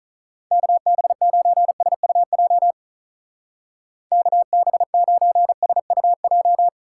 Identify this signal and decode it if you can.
Morse code, heard as a keyed tone, about 44 wpm, 700 Hz. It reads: KB9SUJ KB9SUJ
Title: Speed Racing - USA 2x3 Call Signs 35wpm